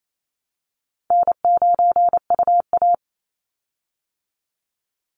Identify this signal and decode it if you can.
Morse code, heard as a keyed tone, about 28 wpm, 700 Hz. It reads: N9UA
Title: Easy Call Signs Repeated 2x - 28wpm